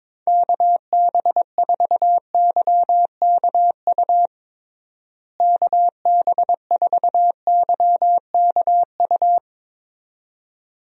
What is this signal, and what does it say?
Morse code, heard as a keyed tone, about 22 wpm, 700 Hz. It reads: KB4YKU KB4YKU
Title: USA Call Signs - 6 Characters - Repeated 2x - 22wpm